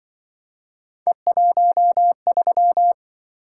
Morse code, heard as a keyed tone, about 24 wpm, 700 Hz. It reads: E13
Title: Speed-Racing - Three Letter-Number Combinations - 20wpm